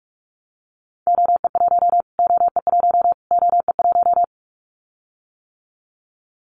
Morse code, heard as a keyed tone, about 44 wpm, 700 Hz. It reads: OE1 OE1 OE1
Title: Callsign Prefixes for Top 25 DXCC - Repeated 3x - 45wpm